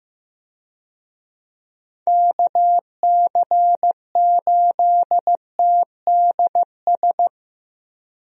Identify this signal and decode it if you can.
Morse code, heard as a keyed tone, about 15 wpm, 700 Hz. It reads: KC8TDS
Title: USA Call Signs - 6 Characters - Repeated 3x - 15wpm